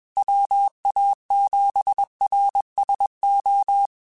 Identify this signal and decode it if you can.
Morse code, heard as a keyed tone, about 21 wpm, 800 Hz. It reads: WA7RSO
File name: QSOs / QSOs-mp3s / kc7sac@20